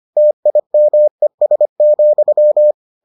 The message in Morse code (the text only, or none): TIMES,